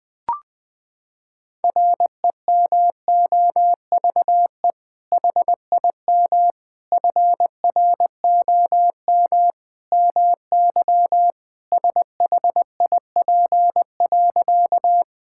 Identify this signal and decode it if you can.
Morse code, heard as a keyed tone, about 21 wpm, 700 Hz. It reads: REMOVE HIM FROM MY SHIP.
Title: Sentences from Top 1000 Words - Review Part 5 - 20wpm